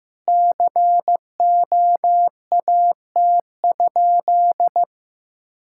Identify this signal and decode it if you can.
Morse code, heard as a keyed tone, about 15 wpm, 700 Hz. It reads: COAT?
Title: Sentences from Top 1600 Words 15wpm